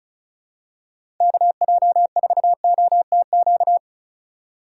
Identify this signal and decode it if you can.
Morse code, heard as a keyed tone, about 36 wpm, 700 Hz. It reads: KJ4OTQ
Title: USA Call Signs - 6 Characters - 8x Character Spacing - 35wpm